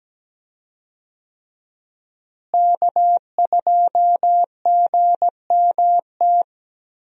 Morse code, heard as a keyed tone, about 17 wpm, 700 Hz. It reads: K2GMT
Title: USA Call Signs - 5 Characters - Repeated 3x - 17wpm